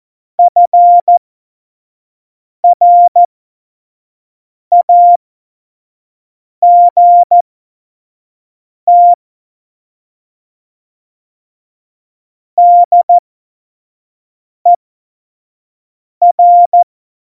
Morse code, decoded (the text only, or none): FRAGT DER